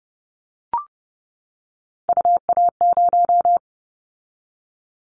Morse code, heard as a keyed tone, about 30 wpm, 700 Hz. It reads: UA0
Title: Callsign Prefixes for Top 100 DXCC - 30wpm